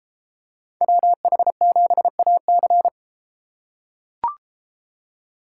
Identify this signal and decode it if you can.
Morse code, heard as a keyed tone, about 32 wpm, 700 Hz. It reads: WH7AC